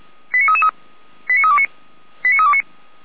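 Five-tone selective call (ZVEI), heard as quick runs of five tones, tones of about 70 ms. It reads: 89333 89329 89329
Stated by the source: ZVEI five-tone call received by SDR.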